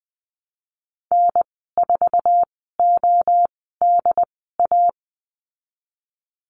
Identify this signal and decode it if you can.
Morse code, heard as a keyed tone, about 20 wpm, spaced as for 15 wpm, 700 Hz. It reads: N4ODA
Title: 076 Lesson 13g - Letter L - Call Signs 20wpm